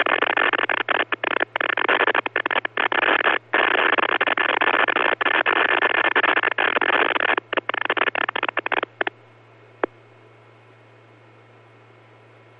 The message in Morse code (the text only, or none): none